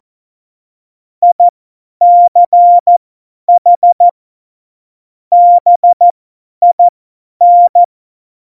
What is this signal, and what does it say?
Morse code, heard as a keyed tone, about 14 wpm, spaced as for 10 wpm, 700 Hz. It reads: ICH BIN
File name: Morsekurs / Witze / Witz 26. Januar 2021, 50 F70